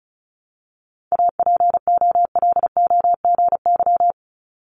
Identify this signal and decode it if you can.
Morse code, heard as a keyed tone, about 36 wpm, 700 Hz. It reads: APOLOGY
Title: Multi-letter Phonemes 35wpm